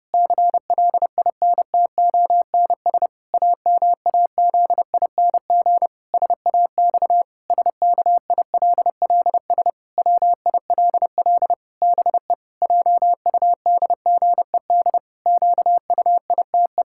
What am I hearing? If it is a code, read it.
Morse code, heard as a keyed tone, about 30 wpm, 700 Hz. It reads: CLINTONS AMAZING SAX SKILLS WILL BE JUDGED QUITE
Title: Pangram Challenge v2 - 30wpm